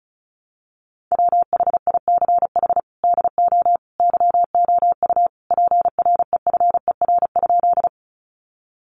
Morse code, heard as a keyed tone, about 34 wpm, 700 Hz. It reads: WHICH DO YOU PREFER?
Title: Sentences from Top 1100 Words 35wpm